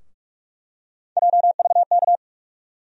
Morse code, heard as a keyed tone, about 46 wpm, 700 Hz. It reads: JVK
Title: USA Call Sign Postfixes - 3 Letters 45wpm